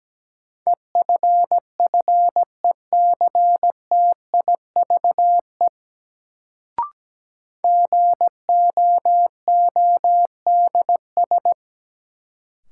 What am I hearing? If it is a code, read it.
Morse code, heard as a keyed tone, about 17 wpm, 700 Hz. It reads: EFFECTIVE GOODS
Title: Sentences from Top 1000 Words - Review Part 6 - New Word Drill - 17wpm